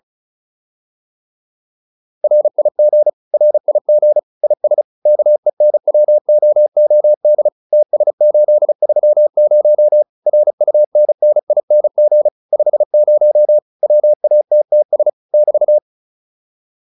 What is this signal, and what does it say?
Morse code, heard as a keyed tone, about 34 wpm, 600 Hz. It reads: RIG RIG IS KENWOOD TS830 RUNNING 50 WATTS <BT>
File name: QSO406 35